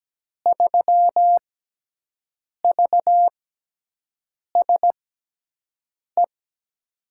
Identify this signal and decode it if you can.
Morse code, heard as a keyed tone, about 17 wpm, spaced as for 6 wpm, 700 Hz. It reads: 3VSE
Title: USA Call Signs - 5 Characters - 6x Character Spacing - 17wpm